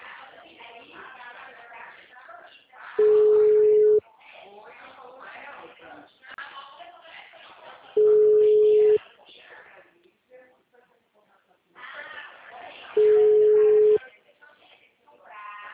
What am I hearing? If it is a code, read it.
Ringback tone (Europe, 425 Hz), with speech.
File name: out- -2010- - - .22251